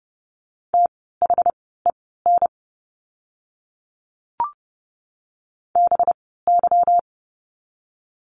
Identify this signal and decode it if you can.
Morse code, heard as a keyed tone, about 30 wpm, spaced as for 17 wpm, 700 Hz. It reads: THEN BY
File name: course / 30:10wpm / 237 Lesson 38f - Letter q - Words 30wpm